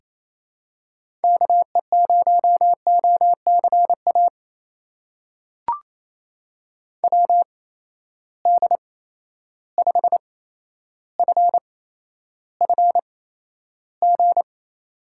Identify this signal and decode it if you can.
Morse code, heard as a keyed tone, about 28 wpm, spaced as for 12 wpm, 700 Hz. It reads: KE0OCA WD5FFG
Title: USA Call Signs - 6 Characters - 8x Character Spacing - 28wpm